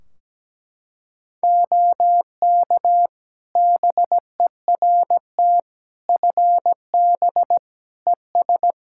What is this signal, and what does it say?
Morse code, heard as a keyed tone, about 17 wpm, 700 Hz. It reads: OK BERT FB ES
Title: Practice QSO - Exchange 2 - Info - 17wpm